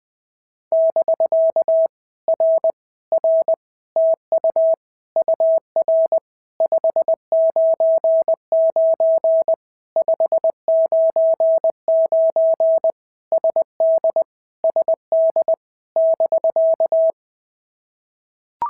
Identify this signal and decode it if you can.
Morse code, heard as a keyed tone, about 20 wpm, 650 Hz. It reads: <BK> R R TU UR 599 599 SD SD <BK>
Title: POTA QSO Practice - 20wpm